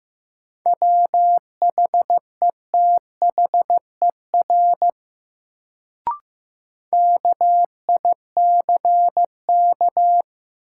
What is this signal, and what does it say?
Morse code, heard as a keyed tone, about 15 wpm, 700 Hz. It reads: WHETHER KICK
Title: Sentences from Top 1300 Words - New Word Drill Alt - 15wpm